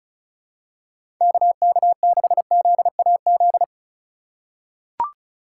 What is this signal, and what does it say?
Morse code, heard as a keyed tone, about 36 wpm, 700 Hz. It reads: KK6ZAZ
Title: USA Call Signs - 3x Character Spacing - 35wpm